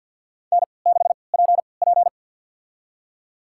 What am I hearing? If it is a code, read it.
Morse code, heard as a keyed tone, about 51 wpm, spaced as for 28 wpm, 700 Hz. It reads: N6PP
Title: USA Call Signs - 4 Characters - 3x Character Spacing - 50wpm